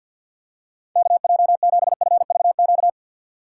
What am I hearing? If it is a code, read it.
Morse code, heard as a keyed tone, about 53 wpm, 700 Hz. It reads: KJ7FVQ
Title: USA Call Signs - 6 Characters - 4x Character Spacing - 50wpm